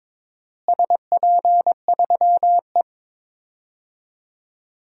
Morse code, heard as a keyed tone, about 21 wpm, 700 Hz. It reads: SP3E
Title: DXCC-100 Call Signs - 4 Characters - Repeated 2x - 22wpm